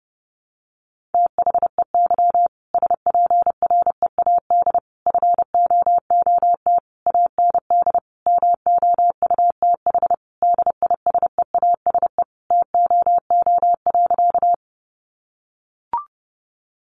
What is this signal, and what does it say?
Morse code, heard as a keyed tone, about 30 wpm, 700 Hz. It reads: THEY SPREAD FOOT AND MOUTH DISEASE TOO.
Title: Sentences from Top 1500 Words 30wpm